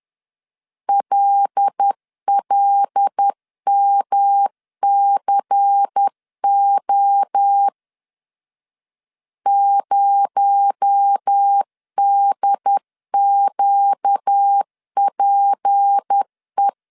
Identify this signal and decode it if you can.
Morse code, heard as a keyed tone, about 10 wpm, 800 Hz. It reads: LLMCO 0DQPE
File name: CW / DET / 1 6 1500C S C 10wpm 4252